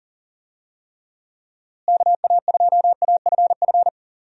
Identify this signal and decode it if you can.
Morse code, heard as a keyed tone, about 37 wpm, 700 Hz. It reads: KA2AFF
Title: USA Call Signs - 6 Characters - Repeated 3x - 40wpm